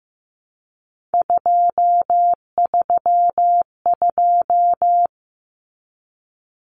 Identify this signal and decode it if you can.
Morse code, heard as a keyed tone, about 15 wpm, 700 Hz. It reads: 232
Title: Common Cut Numbers - 3-Digit 15wpm